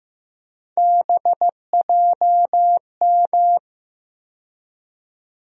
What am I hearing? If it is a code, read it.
Morse code, heard as a keyed tone, about 15 wpm, 700 Hz. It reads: BJM
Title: USA Call Sign Postfixes - 3 Letters 15wpm